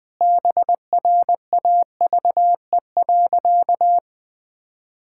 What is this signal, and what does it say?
Morse code, heard as a keyed tone, about 20 wpm, 700 Hz. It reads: BRAVE.
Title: Sentences with 8-Word Phrases - 20wpm